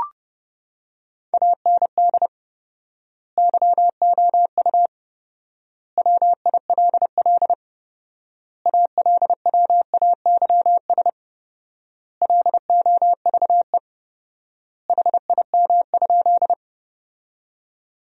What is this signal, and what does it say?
Morse code, heard as a keyed tone, about 30 wpm, 700 Hz. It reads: AND YOU WILL ALWAYS LOVE HIM?
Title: 4x Word-Spacing - Sentences from Top 400 Words - 30wpm